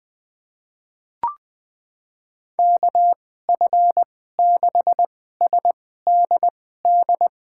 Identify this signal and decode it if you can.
Morse code, heard as a keyed tone, about 21 wpm, spaced as for 15 wpm, 700 Hz. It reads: KF6SDD
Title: USA Call Signs - 2x Character Spacing - 20wpm